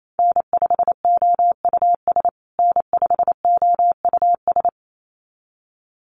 Morse code, heard as a keyed tone, about 28 wpm, 700 Hz. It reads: N5OUS N5OUS
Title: USA Call Signs Repeated 2x 28wpm